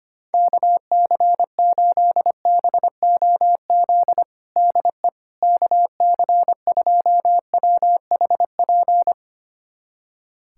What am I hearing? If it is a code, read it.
Morse code, heard as a keyed tone, about 25 wpm, 700 Hz. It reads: KC8BOZ DE KC2WHP
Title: USA Callsign DE Callsign - 6 Characters - 25wpm